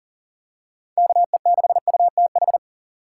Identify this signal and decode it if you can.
Morse code, heard as a keyed tone, about 37 wpm, 700 Hz. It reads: KE6UTH
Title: USA Call Signs - 6 Characters - 3x Character Spacing - 40wpm